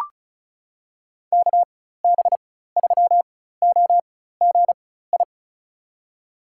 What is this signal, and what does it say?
Morse code, heard as a keyed tone, about 36 wpm, spaced as for 16 wpm, 700 Hz. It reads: KB3OGI
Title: USA Call Signs - 6 Characters - 4x Character Spacing - 35wpm